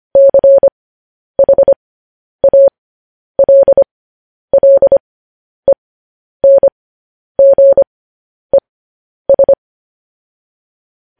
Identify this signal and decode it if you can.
Morse code, heard as a keyed tone, about 25 wpm, spaced as for 10 wpm, 550 Hz. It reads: CHALLENGES
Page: news-headlines-in-morse-code-at-10-wpm